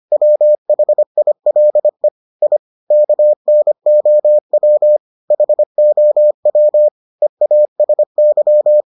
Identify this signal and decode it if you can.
Morse code, heard as a keyed tone, about 25 wpm, 600 Hz. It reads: WHILE I KNOW HOW EASY